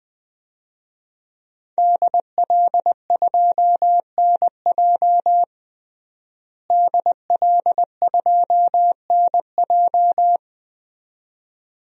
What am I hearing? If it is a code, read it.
Morse code, heard as a keyed tone, about 21 wpm, 700 Hz. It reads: DL2NJ DL2NJ
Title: DXCC-100 Call Signs - 5 Characters - Repeated 2x - 20wpm